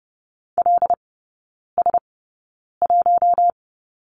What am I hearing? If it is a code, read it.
Morse code, heard as a keyed tone, about 30 wpm, spaced as for 9 wpm, 700 Hz. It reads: LS1
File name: course / 30:10wpm / 087 Lesson 15e - Number 5 - Three Characters 30wpm